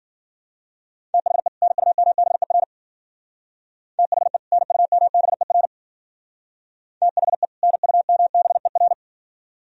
Morse code, heard as a keyed tone, about 53 wpm, 700 Hz. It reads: THE NUMBER THE NUMBER THE NUMBER